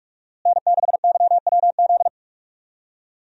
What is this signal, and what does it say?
Morse code, heard as a keyed tone, about 47 wpm, 700 Hz. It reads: N6YWZ
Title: USA Call Signs - 5 Characters - Repeated 2x - 45wpm